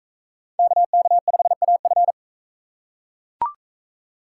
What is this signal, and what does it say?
Morse code, heard as a keyed tone, about 42 wpm, 700 Hz. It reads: KK5AF